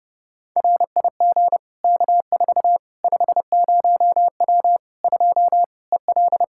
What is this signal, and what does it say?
Morse code, heard as a keyed tone, about 30 wpm, 700 Hz. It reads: RIG K4 50W 2 EL
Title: Practice QSO - Exchange 2 - Info - 30wpm